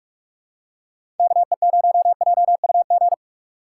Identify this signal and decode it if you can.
Morse code, heard as a keyed tone, about 44 wpm, 700 Hz. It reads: KE0JUG
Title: USA Call Signs - 4x Character Spacing - 45wpm